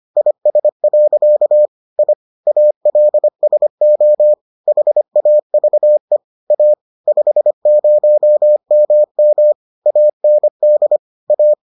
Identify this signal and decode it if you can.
Morse code, heard as a keyed tone, about 25 wpm, 600 Hz. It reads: IS. I ALSO HAVE A 50MM AND A